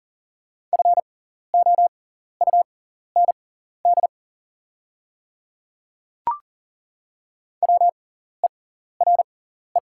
Morse code, heard as a keyed tone, about 40 wpm, spaced as for 14 wpm, 700 Hz. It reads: FOUND WERE
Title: 6x Character Spacing - Top 500 Words - 40wpm